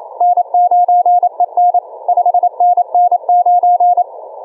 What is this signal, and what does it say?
Morse code, heard as a keyed tone, about 27 wpm, 710 Hz. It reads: N9EN 5NN9
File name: kf6mwa / cqww cw2